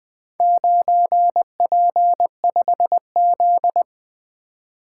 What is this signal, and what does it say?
Morse code, heard as a keyed tone, about 20 wpm, 700 Hz. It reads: 9P5Z
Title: Four Letter-Number Combinations - 20wpm